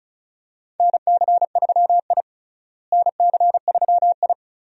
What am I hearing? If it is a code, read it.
Morse code, heard as a keyed tone, about 36 wpm, 700 Hz. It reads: NC3I NC3I
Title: USA Call Signs - 4 Characters - Repeated 2x - 35wpm